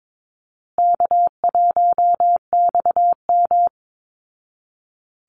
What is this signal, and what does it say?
Morse code, heard as a keyed tone, about 22 wpm, 700 Hz. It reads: K1XM
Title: Call Signs Repeated 3x - 22wpm